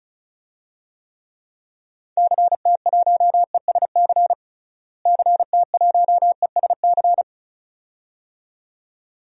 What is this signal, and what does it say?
Morse code, heard as a keyed tone, about 36 wpm, 700 Hz. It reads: CT1ESC CT1ESC
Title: DXCC-100 Call Signs - 6 Characters - Repeated 2x - 35wpm